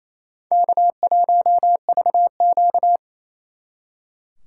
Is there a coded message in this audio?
Yes: Morse code from 0.5 to 3.0 s, heard as a keyed tone, about 35 wpm, 700 Hz. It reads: K1VQ